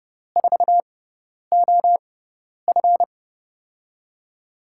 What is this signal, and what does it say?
Morse code, heard as a keyed tone, about 30 wpm, spaced as for 10 wpm, 700 Hz. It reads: 4OF
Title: USA Call Signs - 6x Character Spacing - 30wpm